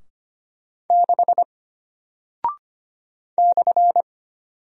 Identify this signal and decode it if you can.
Morse code, heard as a keyed tone, about 25 wpm, 700 Hz. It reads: 6 /
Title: Single Character (Letter:Number:Punctuation:Prosign) Alt 25wpm